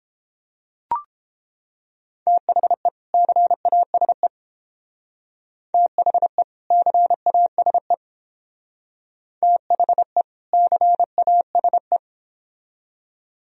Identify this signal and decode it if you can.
Morse code, heard as a keyed tone, about 28 wpm, 700 Hz. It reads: THE CASE THE CASE THE CASE